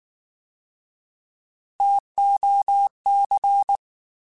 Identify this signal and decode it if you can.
Morse code, heard as a keyed tone, about 19 wpm, 800 Hz. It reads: TOC
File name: Alphabetical / Alphabetical mp3s / Ts @18